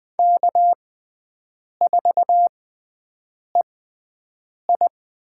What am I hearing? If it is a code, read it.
Morse code, heard as a keyed tone, about 21 wpm, spaced as for 7 wpm, 700 Hz. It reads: K4EI
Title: USA Call Signs - 5 Characters - 6x Character Spacing - 20wpm